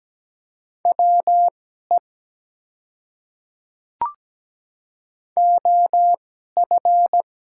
Morse code, heard as a keyed tone, about 17 wpm, spaced as for 12 wpm, 700 Hz. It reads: WE OF
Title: 2x Character Spacing - Top 100 Words - 17wpm